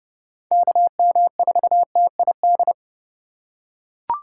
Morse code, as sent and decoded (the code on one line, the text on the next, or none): -.- -- ....- - .. -..
KM4TID